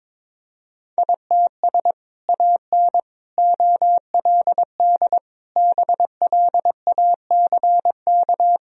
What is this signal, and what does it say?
Morse code, heard as a keyed tone, about 22 wpm, 700 Hz. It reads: ITS AN OLD BLACK
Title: Sentences from Top 1000 Words - Review Part 2 - 22wpm